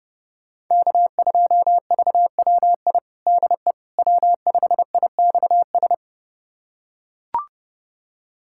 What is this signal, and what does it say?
Morse code, heard as a keyed tone, about 30 wpm, 700 Hz. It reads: K2VWI DE W5IXS
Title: USA Callsign DE Callsign - 5 Characters - 30wpm